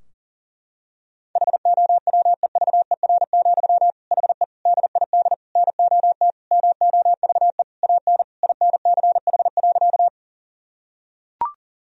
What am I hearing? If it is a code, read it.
Morse code, heard as a keyed tone, about 40 wpm, 700 Hz. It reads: HOWEVER, HE DID NOT MOVE AN INCH.
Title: Sentences from Top 1000 Words - Review Part 3 - 40wpm